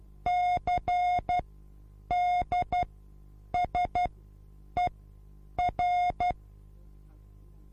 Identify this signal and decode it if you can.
Morse code, heard as a keyed tone, about 12 wpm, spaced as for 8 wpm, 720 Hz. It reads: CDSER